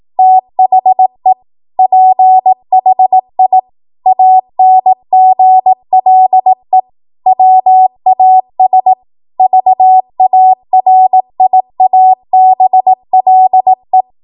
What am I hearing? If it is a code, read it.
Morse code, heard as a keyed tone, about 18 wpm, 750 Hz. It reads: THE PHI ANGLE WAS VARIABLE
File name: Morse / 18WPM